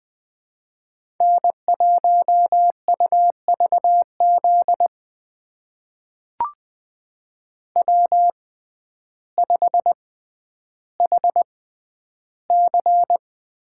Morse code, heard as a keyed tone, about 20 wpm, spaced as for 11 wpm, 700 Hz. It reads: N1UVZ W5HC
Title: USA Call Signs - 5 Characters - 6x Character Spacing - 20wpm